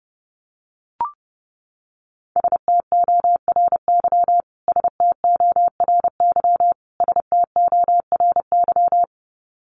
Morse code, heard as a keyed tone, about 30 wpm, 700 Hz. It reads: STORY STORY STORY
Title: Top 300 Words Repeated 30wpm